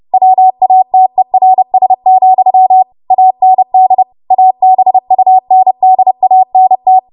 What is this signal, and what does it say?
Morse code, heard as a keyed tone, about 30 wpm, 750 Hz. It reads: WATERS, AND ABUNDANT
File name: Morse / 30WPM